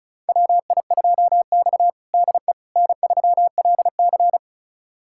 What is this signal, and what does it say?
Morse code, heard as a keyed tone, about 36 wpm, 700 Hz. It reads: WI2X DE N3LC
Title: USA Callsign DE Callsign - 4 Characters - 35wpm